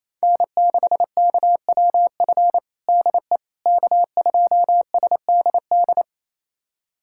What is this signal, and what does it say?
Morse code, heard as a keyed tone, about 27 wpm, 700 Hz. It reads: N6KWF DE K2SDD
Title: USA Callsign DE Callsign - 5 Characters - 28wpm